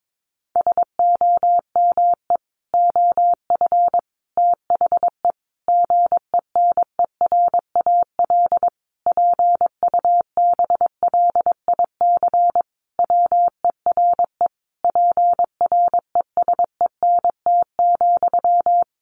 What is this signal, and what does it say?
Morse code, heard as a keyed tone, about 22 wpm, 700 Hz. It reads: SOME OF THE GENERAL PUBLIC WERE PRESENT,
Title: Sentences from Top 800 Words 22wpm